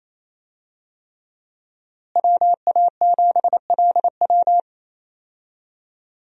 Morse code, heard as a keyed tone, about 28 wpm, 700 Hz. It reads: WA7LW
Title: USA Call Signs - 5 Characters - Repeated 3x - 28wpm